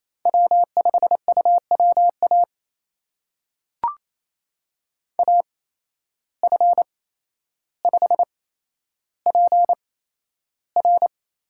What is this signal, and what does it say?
Morse code, heard as a keyed tone, about 28 wpm, spaced as for 12 wpm, 700 Hz. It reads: W5UWA AF5PR
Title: USA Call Signs - 5 Characters - 8x Character Spacing - 28wpm